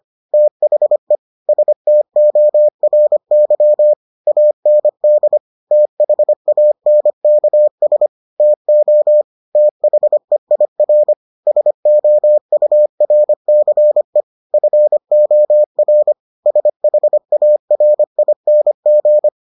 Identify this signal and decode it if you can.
Morse code, heard as a keyed tone, about 25 wpm, 600 Hz. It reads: THE STORY AND THANKS TO THEIR SOURCE FOR SHARING